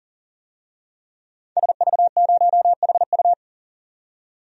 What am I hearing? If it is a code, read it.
Morse code, heard as a keyed tone, about 39 wpm, 700 Hz. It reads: SV0HU
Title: DXCC-100 Call Signs - 5 Characters - 40wpm